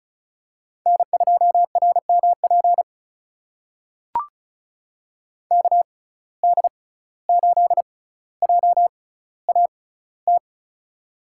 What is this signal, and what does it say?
Morse code, heard as a keyed tone, about 36 wpm, spaced as for 17 wpm, 700 Hz. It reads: N2RMP KD8JAT
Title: USA Call Signs - 6x Character Spacing - 35wpm